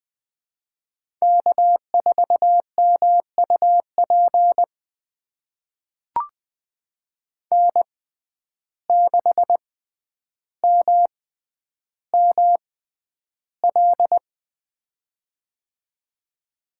Morse code, heard as a keyed tone, about 20 wpm, spaced as for 10 wpm, 700 Hz. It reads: K4MUP N6MML
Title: USA Call Signs - 5 Characters - 6x Character Spacing - 20wpm